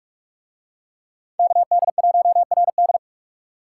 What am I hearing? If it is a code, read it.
Morse code, heard as a keyed tone, about 44 wpm, 700 Hz. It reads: KD1RD